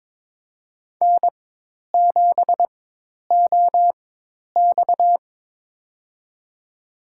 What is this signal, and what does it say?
Morse code, heard as a keyed tone, about 21 wpm, spaced as for 10 wpm, 700 Hz. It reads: N7OX
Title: USA Call Signs - 4 Characters - 4x Character Spacing - 22wpm